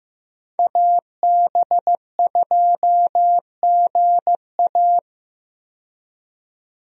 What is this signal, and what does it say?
Morse code, heard as a keyed tone, about 15 wpm, 700 Hz. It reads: AB2GA
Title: USA Call Signs - 5 Characters - 15wpm